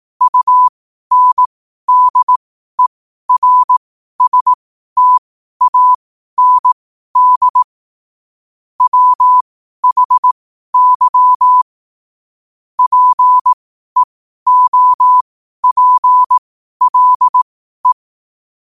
Morse code, heard as a keyed tone, about 18 wpm, spaced as for 12 wpm, 1000 Hz. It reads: UNDERSTAND WHY PEOPLE
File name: qotdfiles / 12wpm